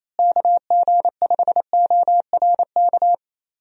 Morse code, heard as a keyed tone, about 28 wpm, 700 Hz. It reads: KG5ORK
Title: USA Call Signs - 6 Characters - Repeated 2x - 28wpm